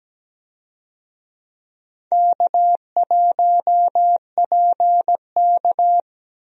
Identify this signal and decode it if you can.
Morse code, heard as a keyed tone, about 17 wpm, 700 Hz. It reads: K1PK
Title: USA Call Signs - 4 Characters - 17wpm